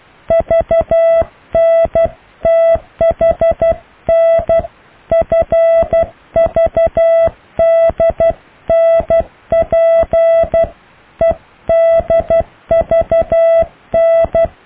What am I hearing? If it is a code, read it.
Morse code, heard as a keyed tone, about 12 wpm, 660 Hz. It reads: VNTHN FVDNP EDVN
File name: cw / EKTPV+BFHNW